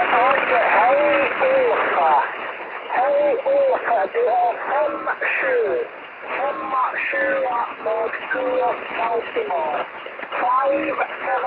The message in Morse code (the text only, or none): none